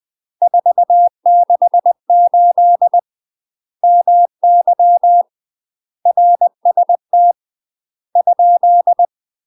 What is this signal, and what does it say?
Morse code, heard as a keyed tone, about 20 wpm, 700 Hz. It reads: 468 MY RST ?
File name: UK MASTER QSOs TRAINING/20WPM